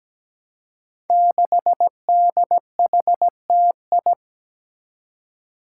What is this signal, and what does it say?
Morse code, heard as a keyed tone, about 17 wpm, 700 Hz. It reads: 6DHTI
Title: Five Challenging Letter-Number Combinations 17wpm